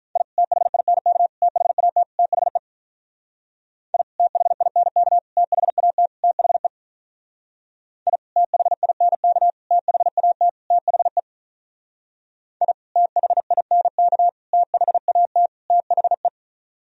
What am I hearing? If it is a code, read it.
Morse code, heard as a keyed tone, about 44 wpm, 700 Hz. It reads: I THINK THAT THE I THINK THAT THE I THINK THAT THE I THINK THAT THE